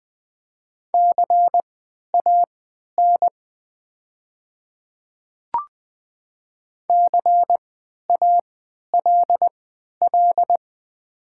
Morse code, heard as a keyed tone, about 20 wpm, spaced as for 11 wpm, 700 Hz. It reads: CAN CALL
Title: 3x Character Spacing - Top 100 Words - 20wpm